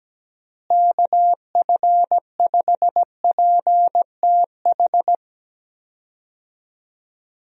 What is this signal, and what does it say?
Morse code, heard as a keyed tone, about 17 wpm, 700 Hz. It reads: KF5PTH
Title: USA Call Signs - 6 Characters - Repeated 2x - 17wpm